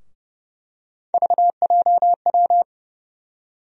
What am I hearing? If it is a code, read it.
Morse code, heard as a keyed tone, about 30 wpm, 700 Hz. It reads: VJW
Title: USA Call Sign Postfixes - 3 Letters 30wpm